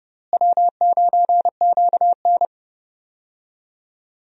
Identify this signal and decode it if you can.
Morse code, heard as a keyed tone, about 30 wpm, 700 Hz. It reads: W9QN
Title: USA Call Signs - 4 Characters - Repeated 2x - 30wpm